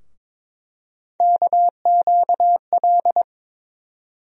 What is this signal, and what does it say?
Morse code, heard as a keyed tone, about 22 wpm, 700 Hz. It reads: KQL
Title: USA Call Sign Postfixes - 3 Letters 22wpm